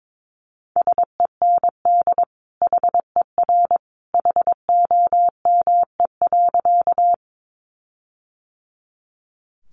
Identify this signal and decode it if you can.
Morse code, heard as a keyed tone, about 22 wpm, 700 Hz. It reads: SEND HER HOME.
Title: Sentences from Top 1500 Words 22wpm